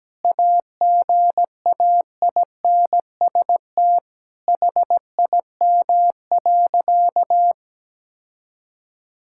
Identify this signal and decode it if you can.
Morse code, heard as a keyed tone, about 17 wpm, 700 Hz. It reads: AGAINST HIM.
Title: Sentences from Top 1600 Words 17wpm